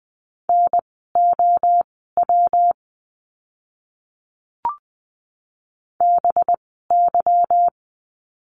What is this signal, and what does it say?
Morse code, heard as a keyed tone, about 20 wpm, spaced as for 14 wpm, 700 Hz. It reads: NOW BY
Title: 230 Lesson 37f - Letter x - Words 20wpm